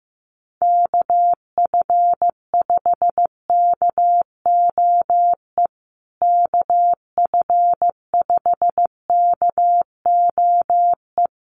Morse code, heard as a keyed tone, about 15 wpm, 700 Hz. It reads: KF5KOE KF5KOE
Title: USA Call Signs Repeated 3x 15wpm